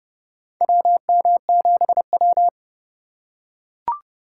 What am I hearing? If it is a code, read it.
Morse code, heard as a keyed tone, about 30 wpm, 700 Hz. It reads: WM7W
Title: USA Call Signs - 4 Characters - 2x Character Spacing - 30wpm